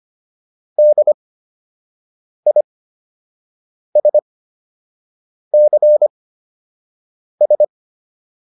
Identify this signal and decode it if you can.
Morse code, heard as a keyed tone, about 25 wpm, spaced as for 6 wpm, 600 Hz. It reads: DISCS